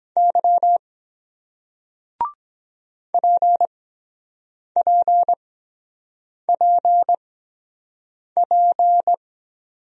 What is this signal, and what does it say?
Morse code, heard as a keyed tone, about 22 wpm, spaced as for 7 wpm, 700 Hz. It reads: Y PPPP